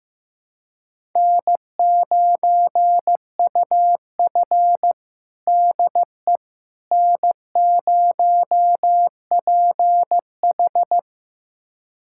Morse code, heard as a keyed tone, about 16 wpm, 700 Hz. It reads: N9UF DE N0PH
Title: USA Callsign DE Callsign - 4 Characters - 15wpm